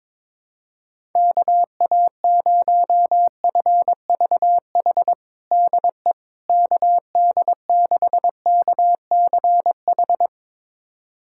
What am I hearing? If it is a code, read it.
Morse code, heard as a keyed tone, about 23 wpm, 700 Hz. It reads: KA0FVH DE KD6KCH
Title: USA Callsign DE Callsign - 6 Characters - 22wpm